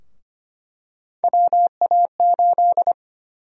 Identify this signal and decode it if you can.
Morse code, heard as a keyed tone, about 25 wpm, 700 Hz. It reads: WA8
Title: USA Call Sign Prefixes - Repeated 2x - 25wpm